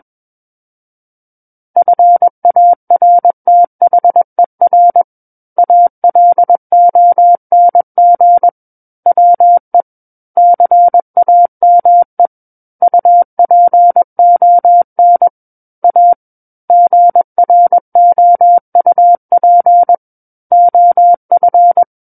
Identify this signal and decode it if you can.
Morse code, heard as a keyed tone, about 21 wpm, 700 Hz. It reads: FARTHER ALONG WE CAME UPON A GROUP OF
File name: Book One / 12-Chapter 12 - 21 WPM